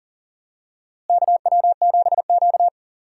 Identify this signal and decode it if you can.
Morse code, heard as a keyed tone, about 40 wpm, 700 Hz. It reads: KW7Q
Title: USA Call Signs - 4 Characters - Repeated 2x - 40wpm